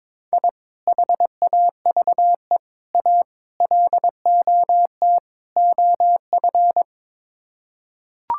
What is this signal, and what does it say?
Morse code, heard as a keyed tone, about 22 wpm, 700 Hz. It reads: I HAVE A LOT OF
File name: practice / 22wpm / Build A Phrase - 5-Word Phrases - 22wpm